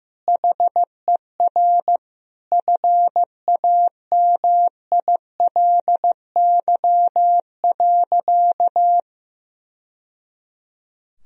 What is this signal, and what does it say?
Morse code, heard as a keyed tone, about 15 wpm, 700 Hz. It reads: HER FAMILY.
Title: Sentences from Top 1000 Words - Review Part 4 - 15wpm